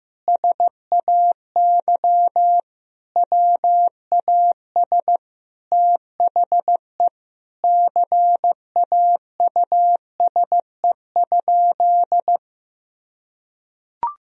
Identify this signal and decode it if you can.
Morse code, heard as a keyed tone, about 15 wpm, 700 Hz. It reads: SAY WAS THE CAUSE?
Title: Sentences from Top 1000 Words - Review Part 1 - 15wpm